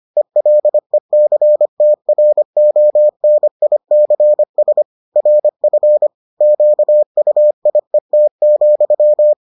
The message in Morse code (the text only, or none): ELECTRONICS RF QUIET,